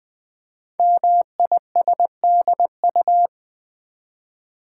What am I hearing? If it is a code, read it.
Morse code, heard as a keyed tone, about 20 wpm, 700 Hz. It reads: MISDU
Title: Code Groups - 5 Letters - 20wpm